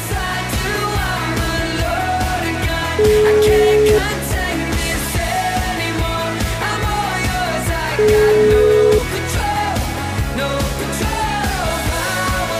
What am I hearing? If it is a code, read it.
Ringback tone (Europe, 425 Hz), with noise.